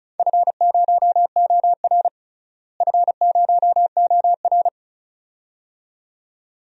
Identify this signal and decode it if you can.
Morse code, heard as a keyed tone, about 34 wpm, 700 Hz. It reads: F0OR F0OR
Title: DXCC-100 Call Signs - 4 Characters - Repeated 2x - 35wpm